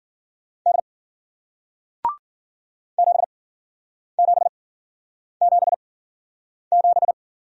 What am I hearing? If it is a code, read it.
Morse code, heard as a keyed tone, about 50 wpm, spaced as for 9 wpm, 700 Hz. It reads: D 7777